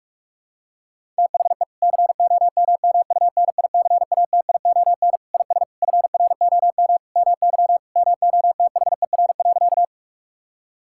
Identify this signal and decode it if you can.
Morse code, heard as a keyed tone, about 41 wpm, 700 Hz. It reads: THE COMMUNICATION IS FROM MY MOTHER.
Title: Sentences from Top 1000 Words - Review Part 6 - 45wpm